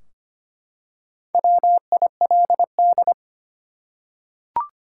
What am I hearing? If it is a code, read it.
Morse code, heard as a keyed tone, about 25 wpm, 700 Hz. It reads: WILD
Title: Four Letter Words - 25wpm